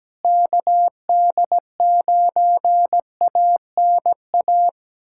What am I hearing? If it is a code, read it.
Morse code, heard as a keyed tone, about 17 wpm, 700 Hz. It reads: KD9ANA
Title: USA Callsign DE Callsign - 6 Characters - 17wpm